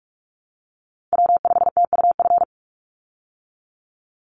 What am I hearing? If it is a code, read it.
Morse code, heard as a keyed tone, about 44 wpm, 700 Hz. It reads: W5TUF
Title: Call Signs Repeated 3x - 45wpm